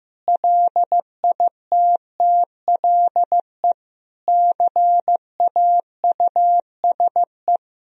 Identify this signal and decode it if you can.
Morse code, heard as a keyed tone, about 16 wpm, 700 Hz. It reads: LITTLE CAUSE
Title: Sentences from Top 1000 Words - Review Part 1 - 15wpm